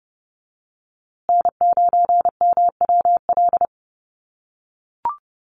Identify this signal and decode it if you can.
Morse code, heard as a keyed tone, about 30 wpm, 700 Hz. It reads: N9MWL
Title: USA Call Signs Repeated 2x 30wpm